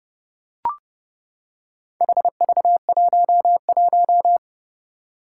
Morse code, heard as a keyed tone, about 30 wpm, 700 Hz. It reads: HV11
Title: Four Challenging Letter-Number Combinations 30wpm